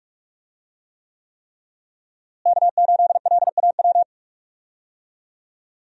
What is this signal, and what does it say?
Morse code, heard as a keyed tone, about 46 wpm, 700 Hz. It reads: K8LAW
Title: USA Call Signs - 5 Characters - Repeated 3x - 45wpm